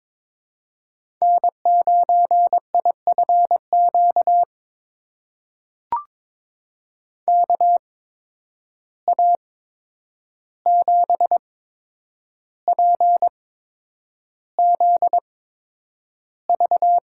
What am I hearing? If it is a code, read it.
Morse code, heard as a keyed tone, about 22 wpm, spaced as for 9 wpm, 700 Hz. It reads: N9IFQ KA7PZV
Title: USA Call Signs - 8x Character Spacing - 22wpm